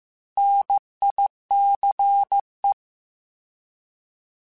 Morse code, heard as a keyed tone, about 15 wpm, 800 Hz. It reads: NICE
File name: Alphabetical / Alphabetical mp3s / Ns @13